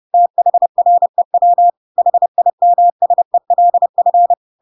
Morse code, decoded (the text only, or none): THREW HIMSELF